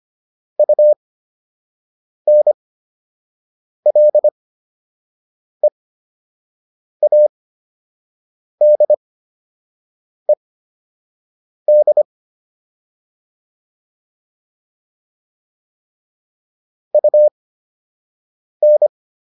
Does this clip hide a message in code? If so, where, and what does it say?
Yes: Morse code from 0.6 to 18.9 s, heard as a keyed tone, about 25 wpm, spaced as for 6 wpm, 600 Hz. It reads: UNLEADED UN